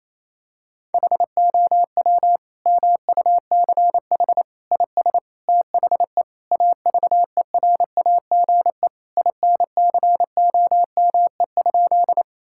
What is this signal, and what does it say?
Morse code, heard as a keyed tone, about 27 wpm, 700 Hz. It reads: HOW MUCH IS THE AVERAGE INCOME?
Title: Sentences from Top 1000 Words - Review Part 6 - 28wpm